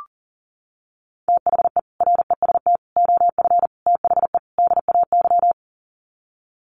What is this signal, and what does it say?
Morse code, heard as a keyed tone, about 40 wpm, 700 Hz. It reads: THE REST OF THE DAY
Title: Sets of 5 Words 40wpm